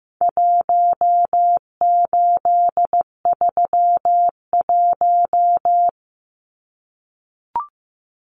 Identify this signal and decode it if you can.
Morse code, heard as a keyed tone, about 15 wpm, 700 Hz. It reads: 1831
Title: Numbers - Contest Style - 4-Digit 15wpm